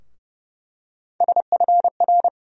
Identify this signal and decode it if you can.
Morse code, heard as a keyed tone, about 30 wpm, 700 Hz. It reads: SFR
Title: USA Call Sign Postfixes - 3 Letters 30wpm